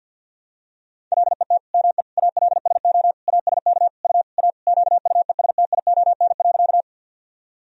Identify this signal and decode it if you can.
Morse code, heard as a keyed tone, about 51 wpm, 700 Hz. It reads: LET ME ALSO ASK U A QUESTION.
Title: Sentences from Top 400 Words and Abbreviations 50wpm